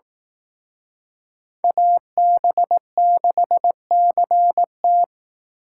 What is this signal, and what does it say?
Morse code, heard as a keyed tone, about 18 wpm, 700 Hz. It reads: AB6CT
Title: Speed Racing - USA 2x2 Call Signs 15wpm